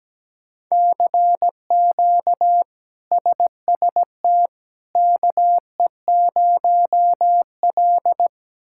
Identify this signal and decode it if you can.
Morse code, heard as a keyed tone, about 17 wpm, 700 Hz. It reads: CQ SST KE0L
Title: K1USN SST QSO Practice - 17wpm